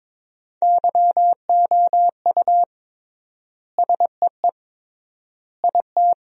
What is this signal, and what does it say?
Morse code, heard as a keyed tone, about 21 wpm, 700 Hz. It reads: YOU SEE IT
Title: 3x Word-Spacing - Sentences from Top 200 Words - 22wpm